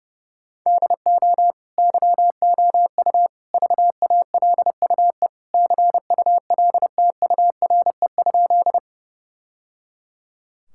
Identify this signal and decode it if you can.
Morse code, heard as a keyed tone, about 30 wpm, 700 Hz. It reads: DO YOU VALUE CULTURE?
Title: Sentences from Top 1000 Words - Review Part 6 - 30wpm